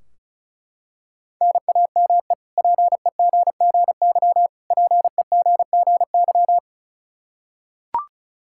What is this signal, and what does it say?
Morse code, heard as a keyed tone, about 34 wpm, 700 Hz. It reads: NAME PEGGY PEGGY
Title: K1USN SST QSO Practice 35wpm